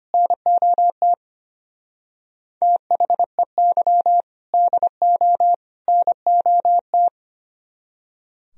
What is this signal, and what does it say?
Morse code, heard as a keyed tone, about 24 wpm, 700 Hz. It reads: NOT THEY DO NOT
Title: Speed-Racing - Sets of 3 Words 25wpm